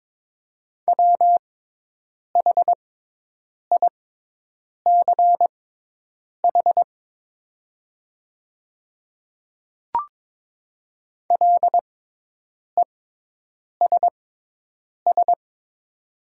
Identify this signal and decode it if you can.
Morse code, heard as a keyed tone, about 22 wpm, spaced as for 8 wpm, 700 Hz. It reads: WHICH LESS
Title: 6x Character Spacing - Top 500 Words - 22wpm